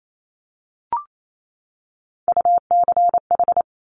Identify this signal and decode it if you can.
Morse code, heard as a keyed tone, about 28 wpm, 700 Hz. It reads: UCH
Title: Top Three Letter Combinations in Words 28wpm